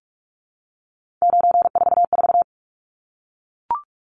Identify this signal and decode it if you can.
Morse code, heard as a keyed tone, about 44 wpm, 700 Hz. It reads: 944
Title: Numbers - 3-Digit 45wpm